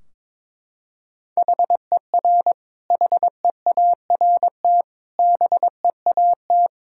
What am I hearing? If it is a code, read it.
Morse code, heard as a keyed tone, about 22 wpm, 700 Hz. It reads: HER HEART BEAT
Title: Sentences from Top 1600 Words 22wpm